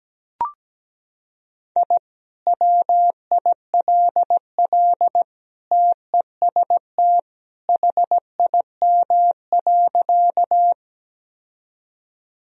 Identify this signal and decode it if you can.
Morse code, heard as a keyed tone, about 17 wpm, 700 Hz. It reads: I WILL TEST HIM.
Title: Sentences from Top 1000 Words - Review Part 3 - 17wpm